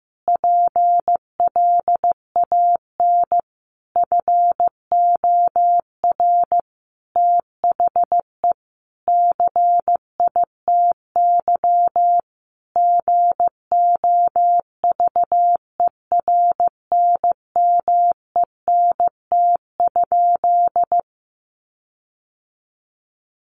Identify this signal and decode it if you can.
Morse code, heard as a keyed tone, about 15 wpm, 700 Hz. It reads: PLAN FOR THE CITY GOVERNMENT?
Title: Sentences from Top 1400 Words 15wpm